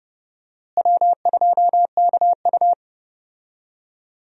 Morse code, heard as a keyed tone, about 30 wpm, 700 Hz. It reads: W2KU
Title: USA Call Signs - 4 Characters - Repeated 3x - 30wpm